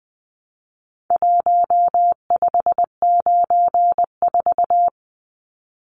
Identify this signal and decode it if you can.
Morse code, heard as a keyed tone, about 20 wpm, 700 Hz. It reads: 1594
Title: Numbers - Contest Style - 4-Digit 20wpm